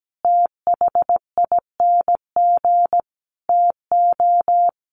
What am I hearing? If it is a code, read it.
Morse code, heard as a keyed tone, about 17 wpm, 700 Hz. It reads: THING TO
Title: Sentences from Top 1300 Words 17wpm